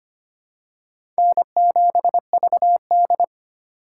Morse code, heard as a keyed tone, about 25 wpm, 700 Hz. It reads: N7VD
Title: USA Call Signs - 2x Character Spacing - 25wpm